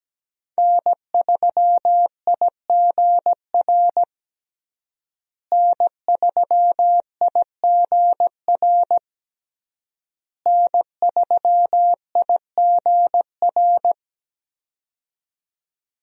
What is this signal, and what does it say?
Morse code, heard as a keyed tone, about 17 wpm, 700 Hz. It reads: N3IGR N3IGR N3IGR
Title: USA Call Signs - 5 Characters - Repeated 3x - 17wpm